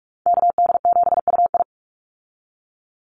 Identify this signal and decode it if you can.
Morse code, heard as a keyed tone, about 44 wpm, 700 Hz. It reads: KD7UI
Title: Call Signs Repeated 2x - 45wpm